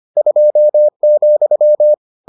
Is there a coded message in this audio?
Yes: Morse code from 0.2 to 1.9 s, heard as a keyed tone, about 25 wpm, 600 Hz. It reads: 2,